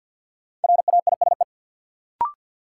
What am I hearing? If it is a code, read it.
Morse code, heard as a keyed tone, about 51 wpm, 700 Hz. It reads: RAISE
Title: Sentences from Top 1400 Words - New Word Drill Alt - 50wpm